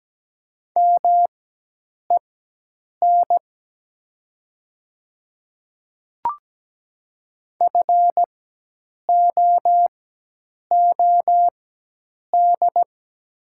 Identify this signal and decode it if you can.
Morse code, heard as a keyed tone, about 17 wpm, spaced as for 8 wpm, 700 Hz. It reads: MEN FOOD
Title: 4x Character Spacing - Top 300 Words - 17wpm